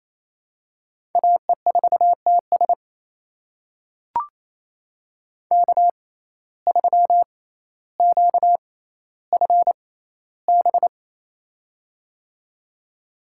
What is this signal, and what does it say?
Morse code, heard as a keyed tone, about 28 wpm, spaced as for 14 wpm, 700 Hz. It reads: AE4TS K3QFB
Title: USA Call Signs - 5 Characters - 6x Character Spacing - 28wpm